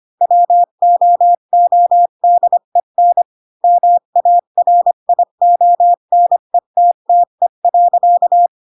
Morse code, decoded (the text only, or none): WOODEN MARIONETTE.